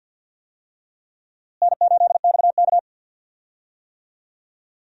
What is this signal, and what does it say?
Morse code, heard as a keyed tone, about 46 wpm, 700 Hz. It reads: N8XK
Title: USA Call Signs - 4 Characters - Repeated 3x - 50wpm